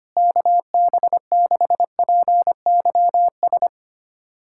Morse code, heard as a keyed tone, about 25 wpm, 700 Hz. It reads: KB6PYS
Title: USA Callsign DE Callsign - 6 Characters - 25wpm